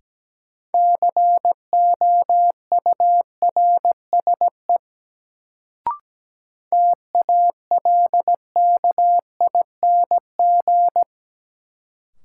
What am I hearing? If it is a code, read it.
Morse code, heard as a keyed tone, about 17 wpm, 700 Hz. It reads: COURSE TALKING
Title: Sets of 2 Words - New Word Drill - 17wpm